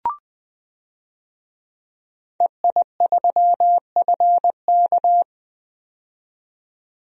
Morse code, heard as a keyed tone, about 20 wpm, 700 Hz. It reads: EI3FK
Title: DXCC-100 Call Signs - 5 Characters - 20wpm